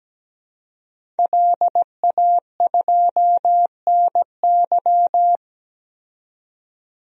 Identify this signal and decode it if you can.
Morse code, heard as a keyed tone, about 17 wpm, 700 Hz. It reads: LA2NY
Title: DXCC-100 Call Signs - 5 Characters - 17wpm